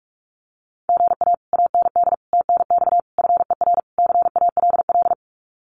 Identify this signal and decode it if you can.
Morse code, heard as a keyed tone, about 45 wpm, 700 Hz. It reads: GA AND TNX FER CALL
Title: Common QSO Phrases Encore 45wpm